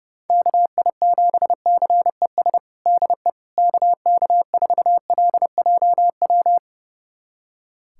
Morse code, heard as a keyed tone, about 30 wpm, 700 Hz. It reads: KI7CES DE KK4LJW
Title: USA Callsign DE Callsign 30wpm